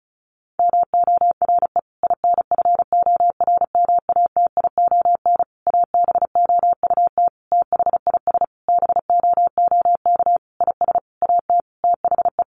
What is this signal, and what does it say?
Morse code, heard as a keyed tone, about 34 wpm, 700 Hz. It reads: MORE INFORMATION ABOUT THIS BOOK IS AT THE
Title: Sentences from Top 1200 Words 35wpm